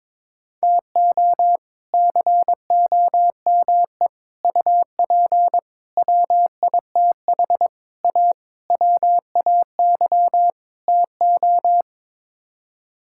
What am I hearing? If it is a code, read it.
Morse code, heard as a keyed tone, about 22 wpm, 700 Hz. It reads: TO COME UP WITH A WAY TO
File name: practice / 22wpm / Sets of 7 Words - 22wpm